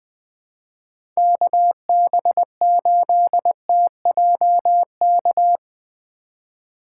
Morse code, heard as a keyed tone, about 20 wpm, 700 Hz. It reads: KB8TJK
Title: USA Call Signs - 6 Characters - 4x Character Spacing - 20wpm